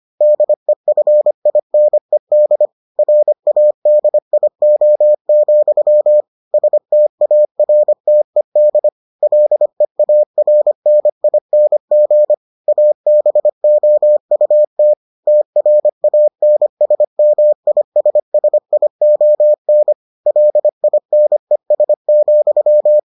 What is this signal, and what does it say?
Morse code, heard as a keyed tone, about 25 wpm, 600 Hz. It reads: DEFINED RADIO, STARTED LEARNING ABOUT TRANSMISSION LINES,